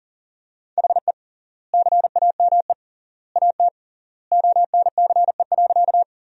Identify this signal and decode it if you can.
Morse code, heard as a keyed tone, about 40 wpm, 700 Hz. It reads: HE CAME AT ONCE.
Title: 3x Word-Spacing - Sentences from Top 300 Words - 40wpm